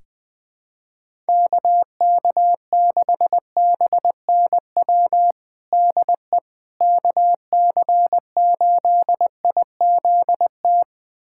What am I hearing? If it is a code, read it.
Morse code, heard as a keyed tone, about 20 wpm, 700 Hz. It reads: KK6BNW DE KC8IZT
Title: USA Callsign DE Callsign 20wpm